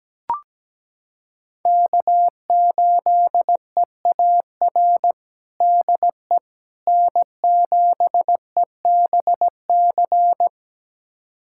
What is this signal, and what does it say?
Morse code, heard as a keyed tone, about 17 wpm, 700 Hz. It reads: K8EAR DE N7EBC
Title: USA Callsign DE Callsign - 5 Characters - 17wpm